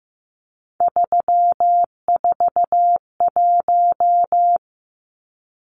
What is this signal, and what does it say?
Morse code, heard as a keyed tone, about 15 wpm, 700 Hz. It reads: 341
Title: Numbers - 3-Digit 15wpm